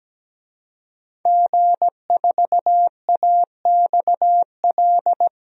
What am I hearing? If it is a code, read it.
Morse code, heard as a keyed tone, about 17 wpm, 700 Hz. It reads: G4AXL
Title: DXCC-100 Call Signs - 5 Characters - Repeated 3x - 17wpm